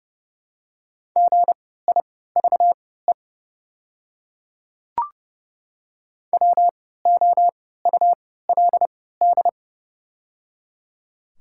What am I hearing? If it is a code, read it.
Morse code, heard as a keyed tone, about 31 wpm, spaced as for 17 wpm, 700 Hz. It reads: GIVE WOULD
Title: 3x Character Spacing - Top 200 Words - 30wpm